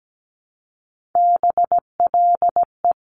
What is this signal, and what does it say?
Morse code, heard as a keyed tone, about 17 wpm, 700 Hz. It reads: BLE
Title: Top Three Letter Combinations in Words 17wpm